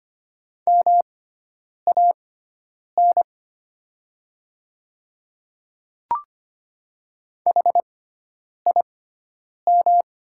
Morse code, heard as a keyed tone, about 25 wpm, spaced as for 9 wpm, 700 Hz. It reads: MAN HIM
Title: 6x Character Spacing - Top 200 Words - 25wpm